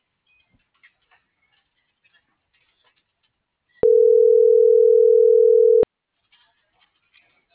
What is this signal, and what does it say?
Ringback tone (North America), 440 + 480 Hz.
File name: CENTRAL MULTIMODAL / out josimar.cruz - .85492